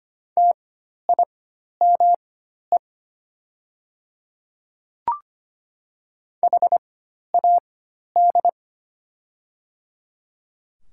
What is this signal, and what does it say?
Morse code, heard as a keyed tone, about 25 wpm, spaced as for 12 wpm, 700 Hz. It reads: TIME HAD
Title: 4x Character Spacing - Top 100 Words - 25wpm